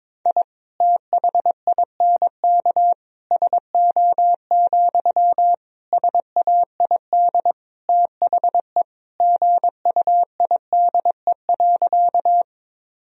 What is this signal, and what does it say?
Morse code, heard as a keyed tone, about 22 wpm, 690 Hz. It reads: I THINK SO, SAID THE GUIDE.
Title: Sentences from Top 1000 Words - Review Part 3 - 22wpm